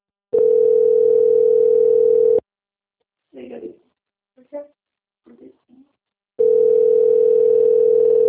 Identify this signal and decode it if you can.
Ringback tone (North America), 440 + 480 Hz.